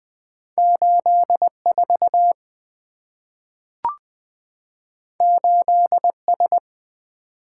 Morse code, heard as a keyed tone, about 20 wpm, 690 Hz. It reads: 84 8S
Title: Two Letter-Number Combinations - 20wpm